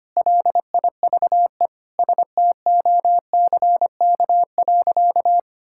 Morse code, heard as a keyed tone, about 25 wpm, 700 Hz. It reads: LIVE STOCK.
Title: Sentences from Top 1000 Words - Review Part 5 - 25wpm